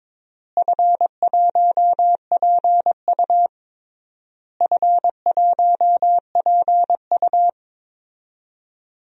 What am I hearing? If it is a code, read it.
Morse code, heard as a keyed tone, about 22 wpm, 700 Hz. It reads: F1PU F1PU
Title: DXCC-100 Call Signs - 4 Characters - Repeated 3x - 22wpm